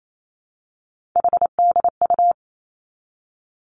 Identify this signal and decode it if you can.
Morse code, heard as a keyed tone, about 28 wpm, 700 Hz. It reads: HDU
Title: Three Challenging Letter-Number Combinations 28wpm